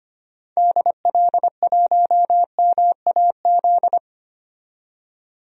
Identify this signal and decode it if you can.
Morse code, heard as a keyed tone, about 25 wpm, 700 Hz. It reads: DL1MAZ
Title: DXCC-100 Call Signs - 6 Characters - 25wpm